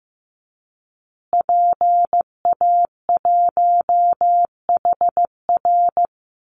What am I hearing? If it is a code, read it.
Morse code, heard as a keyed tone, about 15 wpm, 700 Hz. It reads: PA1HR
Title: Call Signs Repeated 2x - 15wpm